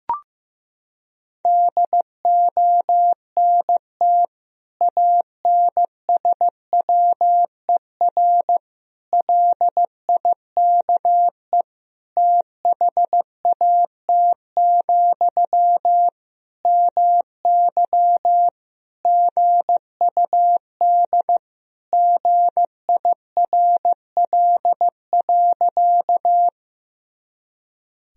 Morse code, heard as a keyed tone, about 15 wpm, 700 Hz. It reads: DONT ANSWER LIKE THAT, MY GUD GIRL.
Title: Sentences from Top 400 Words and Abbreviations 15wpm